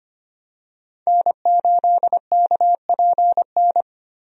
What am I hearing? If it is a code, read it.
Morse code, heard as a keyed tone, about 25 wpm, 700 Hz. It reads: N8KPN
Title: USA Call Signs - 5 Characters - Repeated 3x - 25wpm